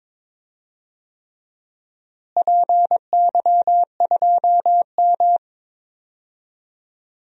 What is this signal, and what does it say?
Morse code, heard as a keyed tone, about 22 wpm, 700 Hz. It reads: PY2M
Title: DXCC-100 Call Signs - 4 Characters - Repeated 3x - 22wpm